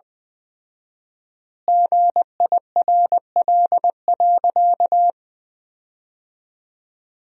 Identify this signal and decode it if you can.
Morse code, heard as a keyed tone, about 21 wpm, 700 Hz. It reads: GIRL.
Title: 4x Word-Spacing - Sentences from Top 400 Words - 20wpm